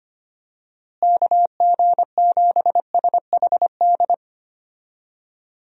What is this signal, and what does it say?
Morse code, heard as a keyed tone, about 25 wpm, 700 Hz. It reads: KG7SHD
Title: USA Call Signs - 6 Characters - Repeated 2x - 25wpm